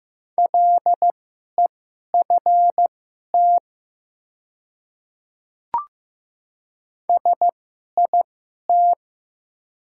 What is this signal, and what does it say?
Morse code, heard as a keyed tone, about 15 wpm, spaced as for 11 wpm, 700 Hz. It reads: LEFT SIT
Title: 2x Character Spacing - Top 500 Words - 15wpm